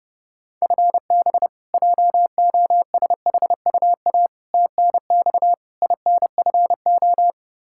Morse code, heard as a keyed tone, about 32 wpm, 700 Hz. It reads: FB JOSHUA TNX INFO
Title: K1USN SST QSO Practice 30wpm